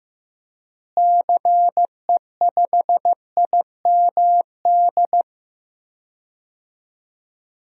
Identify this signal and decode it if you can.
Morse code, heard as a keyed tone, about 15 wpm, 700 Hz. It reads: CE5IMD
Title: DXCC-100 Call Signs - 6 Characters - Repeated 2x - 15wpm